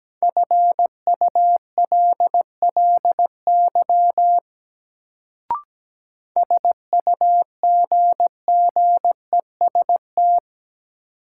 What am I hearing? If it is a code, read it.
Morse code, heard as a keyed tone, about 17 wpm, 700 Hz. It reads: FULLY SUGGEST
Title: Sentences from Top 1300 Words - New Word Drill Alt - 17wpm